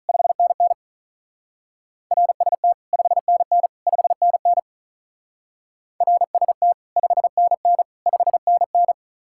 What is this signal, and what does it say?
Morse code, heard as a keyed tone, about 46 wpm, 700 Hz. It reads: 5NN RST 5NN 5NN RST 5NN 5NN